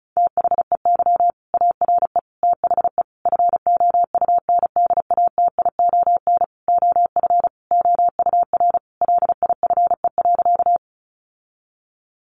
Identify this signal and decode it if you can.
Morse code, heard as a keyed tone, about 34 wpm, 700 Hz. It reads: THEY ARE THE FOUNDATION OF OUR LIFE.
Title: Sentences from Top 1200 Words 35wpm